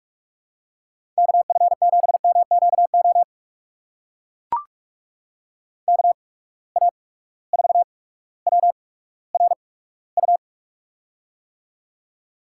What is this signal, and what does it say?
Morse code, heard as a keyed tone, about 43 wpm, spaced as for 19 wpm, 700 Hz. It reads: KF7MQO KA4WRU
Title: USA Call Signs - 6 Characters - 8x Character Spacing - 45wpm